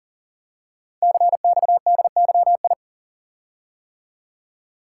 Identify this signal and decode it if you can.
Morse code, heard as a keyed tone, about 42 wpm, 690 Hz. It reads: CXDYI
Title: Code Groups - 5 Letters - 40wpm